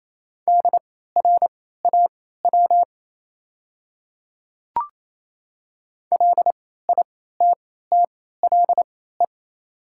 Morse code, heard as a keyed tone, about 27 wpm, spaced as for 16 wpm, 690 Hz. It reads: DRAW LITTLE
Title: 3x Character Spacing - Top 500 Words - 28wpm